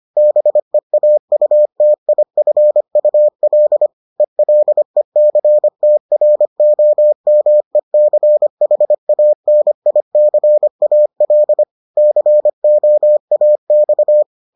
Morse code, heard as a keyed tone, about 25 wpm, 600 Hz. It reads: BEAUTIFUL ELECTROMECHANICAL COAX